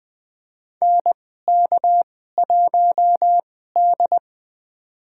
Morse code, heard as a keyed tone, about 20 wpm, spaced as for 15 wpm, 700 Hz. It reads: NK1D
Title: USA Call Signs - 4 Characters - 2x Character Spacing - 20wpm